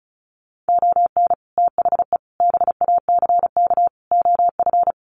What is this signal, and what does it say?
Morse code, heard as a keyed tone, about 34 wpm, 700 Hz. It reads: ON THE BACK OF
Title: Sets of 4 Words 35wpm